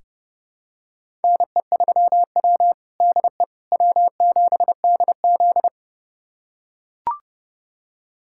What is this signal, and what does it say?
Morse code, heard as a keyed tone, about 32 wpm, 700 Hz. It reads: NE3W DE W7DZ
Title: USA Callsign DE Callsign - 4 Characters - 30wpm